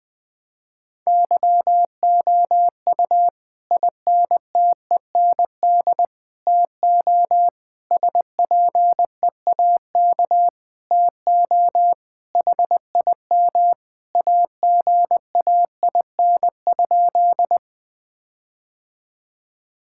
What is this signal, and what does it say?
Morse code, heard as a keyed tone, about 20 wpm, 700 Hz. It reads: YOU INTEND TO SPEAK TO HIM AGAIN?
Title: Sentences from Top 1600 Words 20wpm